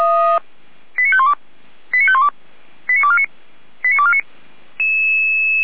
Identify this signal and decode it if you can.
Five-tone selective call (ZVEI), heard as quick runs of five tones, tones of about 70 ms. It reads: 89512 89512 89249 89249
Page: SDR/ZVEI